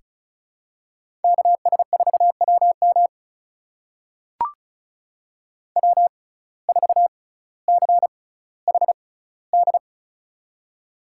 Morse code, heard as a keyed tone, about 36 wpm, spaced as for 18 wpm, 700 Hz. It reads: KS4WM W4CHD
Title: USA Call Signs - 5 Characters - 6x Character Spacing - 35wpm